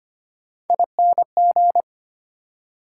Morse code, heard as a keyed tone, about 34 wpm, spaced as for 30 wpm, 700 Hz. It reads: ING
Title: Speed-Racing Prefixes and Suffixes 25wpm